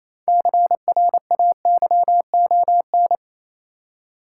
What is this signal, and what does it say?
Morse code, heard as a keyed tone, about 27 wpm, 690 Hz. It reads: CRAYON
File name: Easy Words - Speed-Racing with Hints - 22wpm